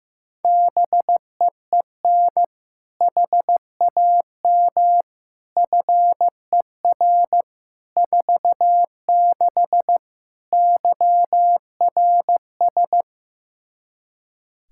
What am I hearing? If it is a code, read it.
Morse code, heard as a keyed tone, about 15 wpm, 700 Hz. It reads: BEEN HAM FER 46 YRS
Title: K1USN SST QSO Practice 15wpm